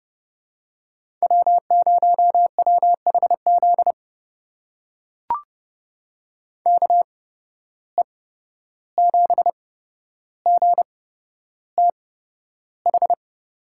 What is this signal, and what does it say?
Morse code, heard as a keyed tone, about 30 wpm, spaced as for 12 wpm, 700 Hz. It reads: W0WHZ KE7GTH
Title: USA Call Signs - 8x Character Spacing - 30wpm